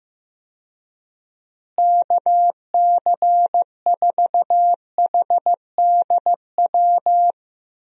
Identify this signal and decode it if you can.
Morse code, heard as a keyed tone, about 15 wpm, 700 Hz. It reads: KC4HDW
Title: USA Call Signs - 6 Characters - Repeated 2x - 15wpm